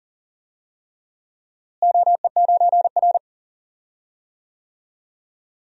Morse code, heard as a keyed tone, about 40 wpm, 700 Hz. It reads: OE9R
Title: DXCC-100 Call Signs - 4 Characters - Repeated 3x - 40wpm